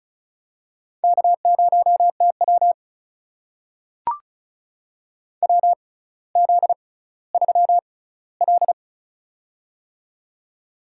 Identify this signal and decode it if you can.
Morse code, heard as a keyed tone, about 36 wpm, spaced as for 18 wpm, 700 Hz. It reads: K0TW WZ3L
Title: USA Call Signs - 4 Characters - 6x Character Spacing - 35wpm